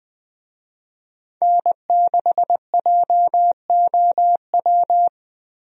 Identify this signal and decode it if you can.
Morse code, heard as a keyed tone, about 20 wpm, 700 Hz. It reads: N6JOW
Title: USA Call Signs - 5 Characters - Repeated 2x - 20wpm